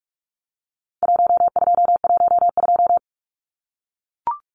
Morse code, heard as a keyed tone, about 44 wpm, 700 Hz. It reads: 1212
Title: Numbers - Contest Style - 4-Digit 45wpm